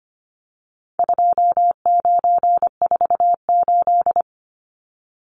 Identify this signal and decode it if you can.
Morse code, heard as a keyed tone, about 25 wpm, 700 Hz. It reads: 2948
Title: Numbers - 4-Digit 25wpm